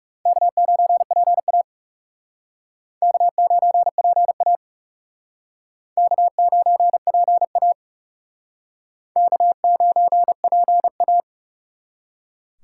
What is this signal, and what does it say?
Morse code, heard as a keyed tone, about 39 wpm, 700 Hz. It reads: K9PA K9PA K9PA K9PA